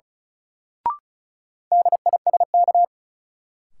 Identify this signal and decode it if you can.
Morse code, heard as a keyed tone, about 36 wpm, 700 Hz. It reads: DISK
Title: Sentences from Top 1000 Words - Review Part 6 - New Word Drill - 35wpm